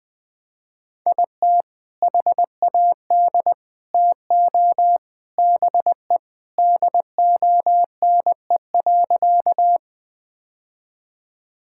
Morse code, heard as a keyed tone, about 20 wpm, 700 Hz. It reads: IT HAD TO BE DONE.
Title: Sentences with 3-Word Phrases - 20wpm